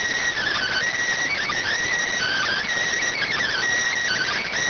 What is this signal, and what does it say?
Emergency Alert System (SAME) tones, heard as harsh buzzing bursts. It is heard from 0.5 to 4.4 s.